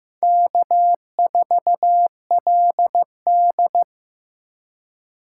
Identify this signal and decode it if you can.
Morse code, heard as a keyed tone, about 15 wpm, 700 Hz. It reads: K4LD
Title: USA Call Signs - 3x Character Spacing - 15wpm